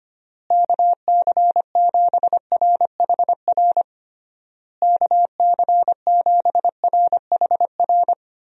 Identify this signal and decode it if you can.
Morse code, heard as a keyed tone, about 25 wpm, 700 Hz. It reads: KC7RHR KC7RHR
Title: USA Call Signs - 6 Characters - Repeated 3x - 25wpm